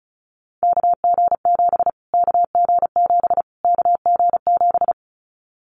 Morse code, heard as a keyed tone, about 34 wpm, 700 Hz. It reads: KG7 KG7 KG7
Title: Callsign Prefixes for Top 25 DXCC - Repeated 3x - 35wpm